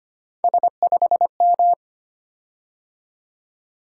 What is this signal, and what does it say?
Morse code, heard as a keyed tone, about 25 wpm, 700 Hz. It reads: S5M
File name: Three Letter-Number Combinations - 25wpm